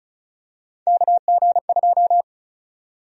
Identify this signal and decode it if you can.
Morse code, heard as a keyed tone, about 34 wpm, 700 Hz. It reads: KG2
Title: USA Call Sign Prefixes - Repeated 2x - 35wpm